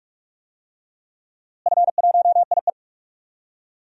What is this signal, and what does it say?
Morse code, heard as a keyed tone, about 47 wpm, 700 Hz. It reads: F1IE
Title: DXCC-100 Call Signs - 4 Characters - Repeated 3x - 45wpm